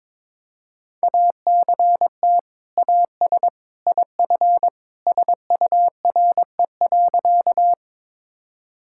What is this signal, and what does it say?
Morse code, heard as a keyed tone, about 22 wpm, 700 Hz. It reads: ACT AS IF SURE.
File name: practice / 22wpm / Sentences with 2-Word Phrases - 22wpm